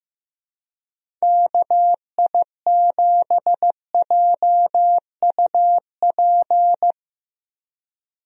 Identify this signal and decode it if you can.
Morse code, heard as a keyed tone, about 15 wpm, 700 Hz. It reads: KI7JUP
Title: USA Call Signs - 6 Characters - Repeated 2x - 15wpm